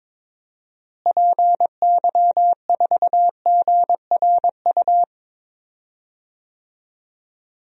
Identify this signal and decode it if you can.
Morse code, heard as a keyed tone, about 22 wpm, 700 Hz. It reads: PY4GRU
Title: DXCC-100 Call Signs - 6 Characters - Repeated 2x - 22wpm